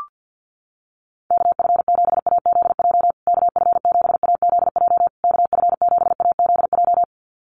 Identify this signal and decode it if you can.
Morse code, heard as a keyed tone, about 50 wpm, 700 Hz. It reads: KF7AZJ KF7AZJ KF7AZJ
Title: USA Call Signs Repeated 3x 50wpm